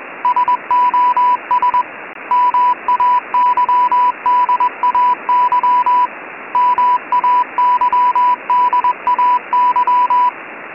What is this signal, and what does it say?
Morse code, heard as a keyed tone, about 21 wpm, 990 Hz, with radio noise. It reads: SOS MAYDAY MAYDAY
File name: Sos Mayday Morse Code 988 Hz 20wpm With Background Static